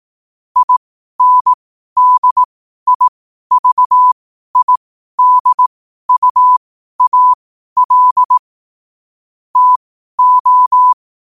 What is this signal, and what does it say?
Morse code, heard as a keyed tone, about 18 wpm, spaced as for 12 wpm, 1000 Hz. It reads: INDIVIDUAL TO
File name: qotdfiles / 12wpm